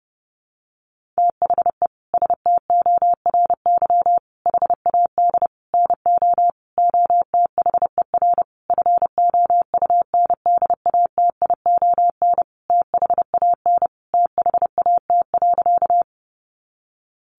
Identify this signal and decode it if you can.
Morse code, heard as a keyed tone, about 30 wpm, 700 Hz. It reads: THE STORY HAD NO OTHER FOUNDATION THAN THAT.
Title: Sentences from Top 1200 Words 30wpm